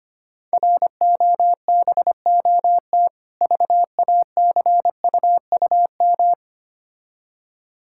Morse code, heard as a keyed tone, about 26 wpm, 700 Hz. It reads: ROBOT VACUUM
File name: Difficult Words - Speed-Racing with Hints - 25wpm